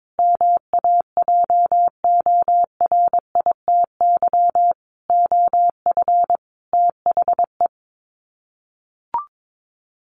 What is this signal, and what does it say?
Morse code, heard as a keyed tone, about 22 wpm, 700 Hz. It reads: MAJORITY OF THE
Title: Sets of 5 Words Encore 22wpm